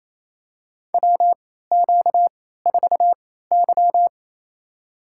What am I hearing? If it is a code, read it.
Morse code, heard as a keyed tone, about 28 wpm, spaced as for 16 wpm, 700 Hz. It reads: WQ4Y
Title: USA Call Signs - 4 Characters - 3x Character Spacing - 28wpm